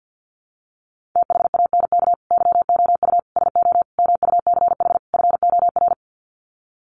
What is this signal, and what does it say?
Morse code, heard as a keyed tone, about 50 wpm, 700 Hz. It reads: THANK YOU SO MUCH FOR
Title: Sets of 5 Words 50wpm